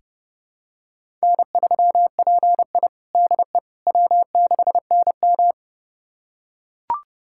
Morse code, heard as a keyed tone, about 32 wpm, 700 Hz. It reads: N3PI DE W6NM
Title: USA Callsign DE Callsign - 4 Characters - 30wpm